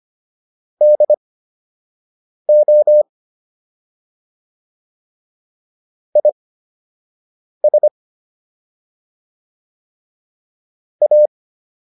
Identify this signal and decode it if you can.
Morse code, heard as a keyed tone, about 25 wpm, spaced as for 6 wpm, 600 Hz. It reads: DO IS A